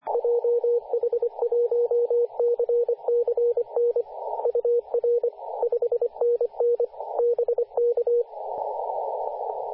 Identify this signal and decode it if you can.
Morse code, heard as a keyed tone, about 25 wpm, 480 Hz, with radio noise. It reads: JH1CCN UR 5NN BK
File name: v633ks-14-023mhz-cw